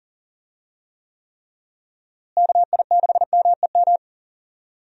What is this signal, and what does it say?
Morse code, heard as a keyed tone, about 40 wpm, 700 Hz. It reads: KI6MEM
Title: USA Call Signs - 6 Characters - Repeated 3x - 40wpm